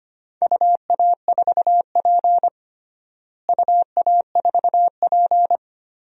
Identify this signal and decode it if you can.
Morse code, heard as a keyed tone, about 25 wpm, 700 Hz. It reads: UA4P UA4P
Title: DXCC-100 Call Signs - 4 Characters - Repeated 2x - 25wpm